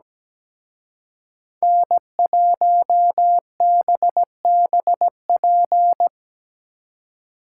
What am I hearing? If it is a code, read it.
Morse code, heard as a keyed tone, about 20 wpm, 700 Hz. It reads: N1BBP
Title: Speed Racing - USA 1x3 Call Signs 17wpm